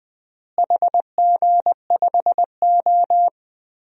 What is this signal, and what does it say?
Morse code, heard as a keyed tone, about 20 wpm, 700 Hz. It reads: HG5O
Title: DXCC-100 Call Signs - 4 Characters - Repeated 2x - 20wpm